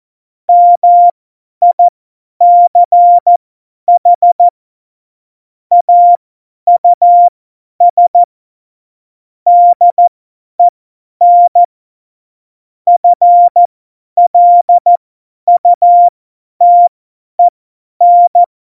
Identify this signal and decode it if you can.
Morse code, heard as a keyed tone, about 14 wpm, spaced as for 10 wpm, 700 Hz. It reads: MICH AUS DEN FLUTEN
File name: Morsekurs / Witze / Witz 11 Juli 17, Tempo 50 F70